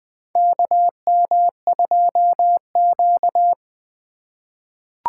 Morse code, heard as a keyed tone, about 20 wpm, 700 Hz. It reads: KM2Q
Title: USA Call Signs - 4 Characters - 3x Character Spacing - 20wpm